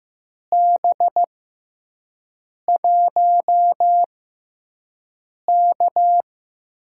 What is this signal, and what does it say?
Morse code, heard as a keyed tone, about 15 wpm, spaced as for 5 wpm, 700 Hz. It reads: B1K
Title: USA Call Signs - 6x Character Spacing - 15wpm